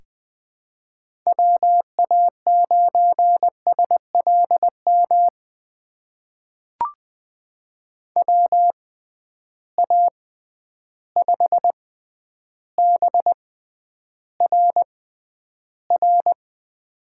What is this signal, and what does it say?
Morse code, heard as a keyed tone, about 20 wpm, spaced as for 10 wpm, 700 Hz. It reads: WA9SLM WA5BRR
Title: USA Call Signs - 6 Characters - 6x Character Spacing - 20wpm